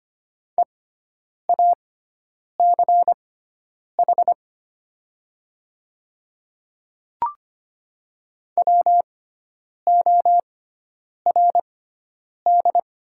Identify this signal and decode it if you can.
Morse code, heard as a keyed tone, about 25 wpm, spaced as for 9 wpm, 700 Hz. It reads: EACH WORD
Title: 6x Character Spacing - Top 200 Words - 25wpm